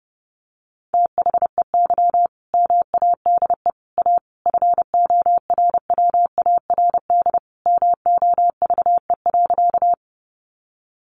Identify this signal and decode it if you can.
Morse code, heard as a keyed tone, about 30 wpm, 700 Hz. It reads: THEY MADE A FORWARD MOVE.
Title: Sentences from Top 1300 Words 30wpm